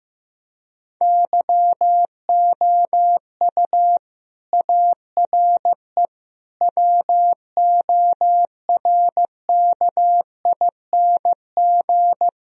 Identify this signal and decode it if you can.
Morse code, heard as a keyed tone, about 16 wpm, 700 Hz. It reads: YOU ARE WORKING
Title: Sentences from Top 1000 Words - Review Part 4 - 15wpm